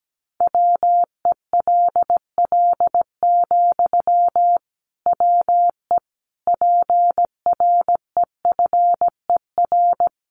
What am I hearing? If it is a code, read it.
Morse code, heard as a keyed tone, about 17 wpm, 700 Hz. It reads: WELL, WE PREFER
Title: Sentences from Top 1100 Words 17wpm